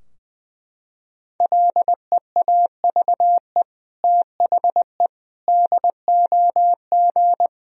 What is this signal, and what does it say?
Morse code, heard as a keyed tone, about 21 wpm, 700 Hz. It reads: LEAVE THE DOG
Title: Sentences from Top 1000 Words - Review Part 2 - 20wpm